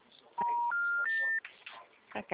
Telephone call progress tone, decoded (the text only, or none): special information tone